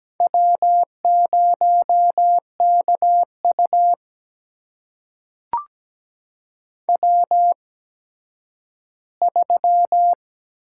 Morse code, heard as a keyed tone, about 17 wpm, spaced as for 10 wpm, 700 Hz. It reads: W0KU W3
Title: USA Call Signs - 4 Characters - 8x Character Spacing - 17wpm